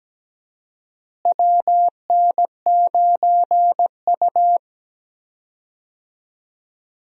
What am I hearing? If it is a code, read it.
Morse code, heard as a keyed tone, about 17 wpm, 700 Hz. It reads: WN9U
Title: USA Call Signs - 4 Characters - Repeated 2x - 17wpm